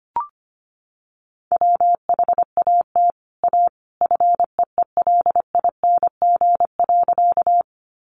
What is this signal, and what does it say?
Morse code, heard as a keyed tone, about 25 wpm, 700 Hz. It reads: WHAT A FEELING.
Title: Sentences from Top 1200 Words 25wpm